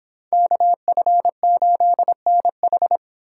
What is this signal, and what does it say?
Morse code, heard as a keyed tone, about 26 wpm, 700 Hz. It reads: KF8NH